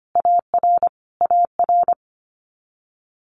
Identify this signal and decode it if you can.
Morse code, heard as a keyed tone, about 25 wpm, 700 Hz. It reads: AR AR
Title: Multi-letter Phonemes Repeated 3x 25wpm